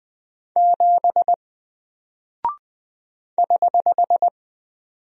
Morse code, heard as a keyed tone, about 20 wpm, 700 Hz. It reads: 7 <HH>
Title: Single Character (Letter:Number:Punctuation:Prosign) Alt 20wpm